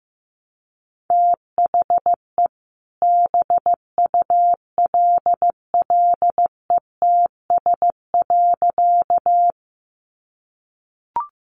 Morse code, heard as a keyed tone, about 15 wpm, 700 Hz. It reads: THE BULLETS.
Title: A Sound Of Thunder 15wpm